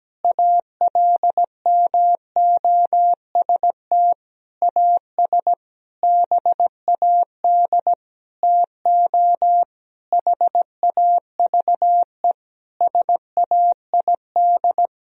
Morse code, heard as a keyed tone, about 17 wpm, 700 Hz. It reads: ALMOST AS BAD TO HAVE SAID
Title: Sentences from Top 1000 Words - Review Part 4 - 17wpm